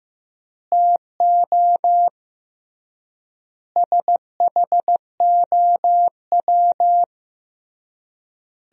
Morse code, heard as a keyed tone, about 14 wpm, 700 Hz. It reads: TO SHOW
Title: 3x Word-Spacing - Sentences from Top 200 Words - 15wpm